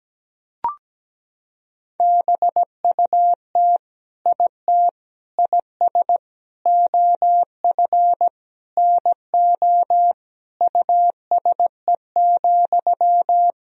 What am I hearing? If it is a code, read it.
Morse code, heard as a keyed tone, about 17 wpm, 700 Hz. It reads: BUT IT IS OF NO USE,
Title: Sentences from Top 1000 Words - Review Part 2 - 17wpm